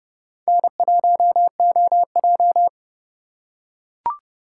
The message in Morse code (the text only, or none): N1OJ